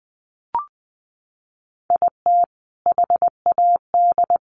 Morse code, heard as a keyed tone, about 20 wpm, 700 Hz. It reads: IT HAD
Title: Sets of 5 Words Encore2 20wpm